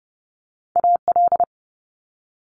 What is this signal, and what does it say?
Morse code, heard as a keyed tone, about 30 wpm, 700 Hz. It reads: AL
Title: Multi-letter Phonemes Quick Fire 30wpm